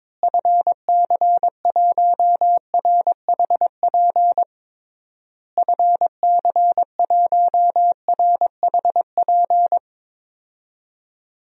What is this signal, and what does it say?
Morse code, heard as a keyed tone, about 22 wpm, 700 Hz. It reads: FC1RHP FC1RHP
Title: DXCC-100 Call Signs - 6 Characters - Repeated 3x - 22wpm